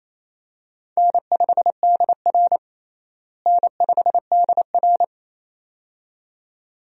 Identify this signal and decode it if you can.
Morse code, heard as a keyed tone, about 28 wpm, 700 Hz. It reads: N5DR N5DR
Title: USA Call Signs - 4 Characters - Repeated 2x - 28wpm